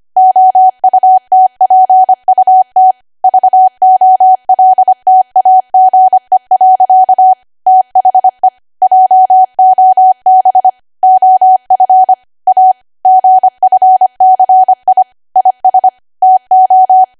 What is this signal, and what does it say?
Morse code, heard as a keyed tone, about 25 wpm, 750 Hz. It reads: OUTPUT VOLTAGE. THE JOB OF A GFCI IS TO
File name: cwa / W1AW-Level3 / 25WPM